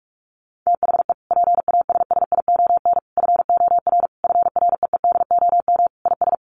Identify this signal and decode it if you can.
Morse code, heard as a keyed tone, about 44 wpm, 700 Hz. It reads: THE PASSION FOR FREEDOM IS
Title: Sentences from Top 1300 Words 45wpm